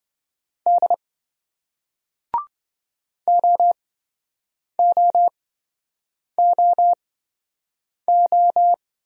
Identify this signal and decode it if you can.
Morse code, heard as a keyed tone, about 27 wpm, spaced as for 7 wpm, 700 Hz. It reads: D OOOO